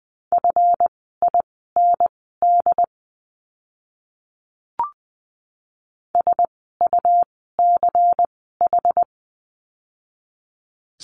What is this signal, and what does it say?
Morse code, heard as a keyed tone, about 20 wpm, spaced as for 14 wpm, 700 Hz. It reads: FIND SUCH
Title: 192 Lesson 31f - Letter v - Words 20wpm